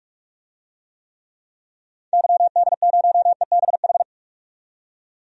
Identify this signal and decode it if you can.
Morse code, heard as a keyed tone, about 46 wpm, 700 Hz. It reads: YD0EBH
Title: DXCC-100 Call Signs - 6 Characters - 45wpm